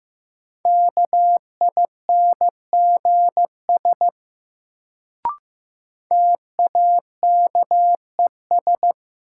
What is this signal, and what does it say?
Morse code, heard as a keyed tone, about 16 wpm, 700 Hz. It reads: KINGS TAKES
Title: Sentences from Top 1000 Words - Review Part 6 - New Word Drill - 15wpm